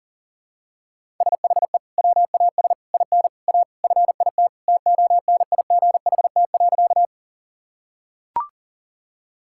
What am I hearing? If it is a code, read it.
Morse code, heard as a keyed tone, about 40 wpm, 700 Hz. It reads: SHE WAS IN A FIT TONIGHT.
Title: Sentences from Top 1000 Words - Review Part 5 - 40wpm